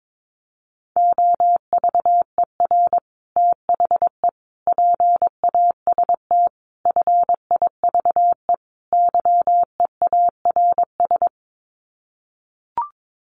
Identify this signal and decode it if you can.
Morse code, heard as a keyed tone, about 22 wpm, 700 Hz. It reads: OVER THE PAST FIVE YEARS
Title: Sets of 5 Words Encore 22wpm